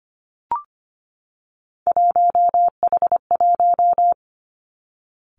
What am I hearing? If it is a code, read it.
Morse code, heard as a keyed tone, about 25 wpm, 700 Hz. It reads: 1H1
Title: Three Challenging Letter-Number Combinations 25wpm